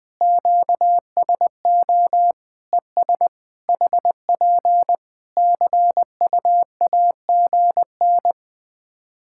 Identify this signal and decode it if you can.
Morse code, heard as a keyed tone, about 20 wpm, 700 Hz. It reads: QSO ES HP CUAGN
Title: K1USN SST QSO Practice 20wpm